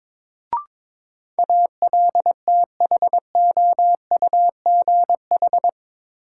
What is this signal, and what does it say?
Morse code, heard as a keyed tone, about 21 wpm, 700 Hz. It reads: ALTHOUGH
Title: Sentences from Top 600 Words - New Word Drill - 22wpm